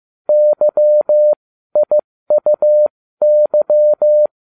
Morse code, heard as a keyed tone, about 15 wpm, spaced as for 13 wpm, 600 Hz. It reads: YIUY